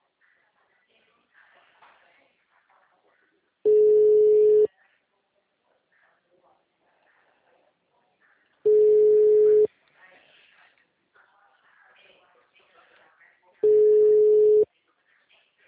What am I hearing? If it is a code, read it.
Ringback tone (Europe, 425 Hz), with speech.